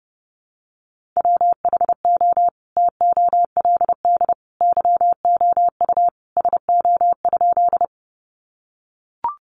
Morse code, heard as a keyed tone, about 30 wpm, 700 Hz. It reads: WHO TOLD YOU SO?
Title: Sentences from Top 400 Words 30wpm